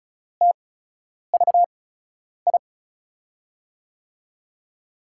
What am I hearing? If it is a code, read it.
Morse code, heard as a keyed tone, about 36 wpm, spaced as for 10 wpm, 700 Hz. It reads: TVI
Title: USA Call Signs - 8x Character Spacing - 35wpm